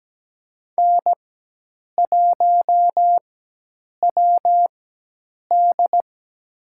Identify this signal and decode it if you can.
Morse code, heard as a keyed tone, about 17 wpm, spaced as for 8 wpm, 700 Hz. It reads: N1WD
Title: USA Call Signs - 4 Characters - 4x Character Spacing - 17wpm